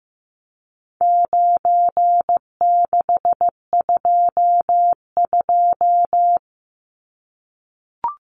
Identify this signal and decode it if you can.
Morse code, heard as a keyed tone, about 15 wpm, 700 Hz. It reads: 9622
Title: Numbers - 4-Digit 15wpm